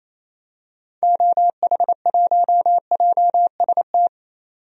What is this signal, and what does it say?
Morse code, heard as a keyed tone, about 28 wpm, 700 Hz. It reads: OH1JST
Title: DXCC-100 Call Signs - 6 Characters - 28wpm